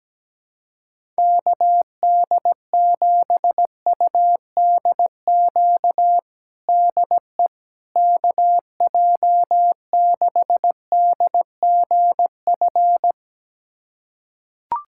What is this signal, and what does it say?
Morse code, heard as a keyed tone, about 17 wpm, 700 Hz. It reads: KD7UDQ DE KJ6DGF
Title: USA Callsign DE Callsign - 6 Characters - 17wpm